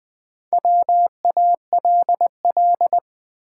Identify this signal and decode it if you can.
Morse code, heard as a keyed tone, about 20 wpm, 700 Hz. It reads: WALL
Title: Sentences from Top 1100 Words - New Word Drill Alt - 20wpm